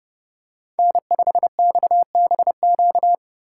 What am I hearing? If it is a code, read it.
Morse code, heard as a keyed tone, about 27 wpm, 700 Hz. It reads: N5XBQ